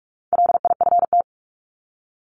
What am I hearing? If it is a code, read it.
Morse code, heard as a keyed tone, about 44 wpm, 700 Hz. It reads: LIFT
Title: Top 751-1000 Words 45wpm